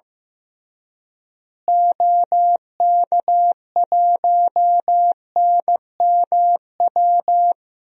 Morse code, heard as a keyed tone, about 15 wpm, 700 Hz. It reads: OK1NMW
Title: DXCC-100 Call Signs - 6 Characters - Repeated 3x - 15wpm